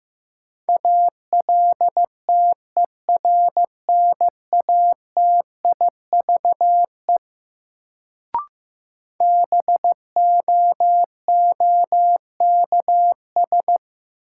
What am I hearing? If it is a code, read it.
Morse code, heard as a keyed tone, about 15 wpm, 700 Hz. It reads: ALTERNATIVE BOOKS
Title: Sentences from Top 1000 Words - Review Part 4 - New Word Drill - 15wpm